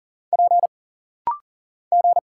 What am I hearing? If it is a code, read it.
Morse code, heard as a keyed tone, about 40 wpm, 700 Hz. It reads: P G
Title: Single Letter-Number - Rapid-Fire 40wpm